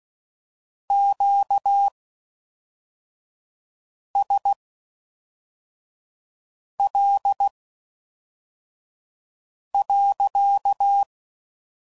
Morse code, heard as a keyed tone, about 16 wpm, spaced as for 4 wpm, 800 Hz. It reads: QSL.